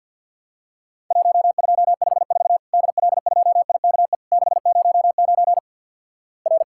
Morse code, heard as a keyed tone, about 50 wpm, 700 Hz. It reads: 1254 DL2IKE 609 R
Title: Worked All Europe DX Contest - QTC Exchange - 50wpm